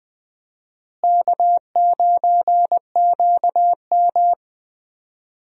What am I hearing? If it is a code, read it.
Morse code, heard as a keyed tone, about 20 wpm, 700 Hz. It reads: K9QM
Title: USA Call Signs - 4 Characters - Repeated 2x - 20wpm